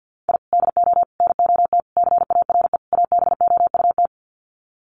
Morse code, heard as a keyed tone, about 50 wpm, 700 Hz. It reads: I DO NOT CARE ABOUT
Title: Sets of 5 Words Encore2 50wpm